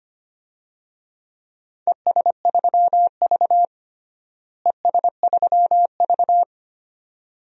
Morse code, heard as a keyed tone, about 25 wpm, 700 Hz. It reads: ES3V ES3V
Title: DXCC-100 Call Signs - 4 Characters - Repeated 2x - 25wpm